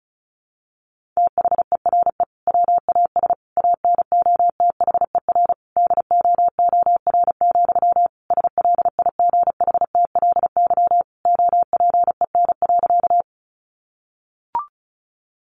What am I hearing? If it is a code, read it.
Morse code, heard as a keyed tone, about 34 wpm, 700 Hz. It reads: THERE WAS ANOTHER DOOR, SLIGHTLY OPEN.
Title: Sentences from Top 1300 Words 35wpm